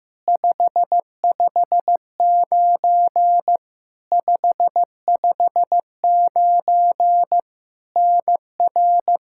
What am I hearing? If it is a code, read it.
Morse code, heard as a keyed tone, about 15 wpm, 700 Hz. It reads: 559 559 NR
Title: Practice QSO - Exchange 1 - Beginning - 15wpm